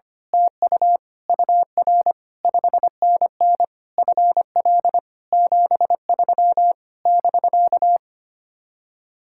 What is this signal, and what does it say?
Morse code, heard as a keyed tone, about 24 wpm, 700 Hz. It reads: TU UR 5NN FL 73 <BK>
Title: Speed-Racing Common QSO Phrases - 25wpm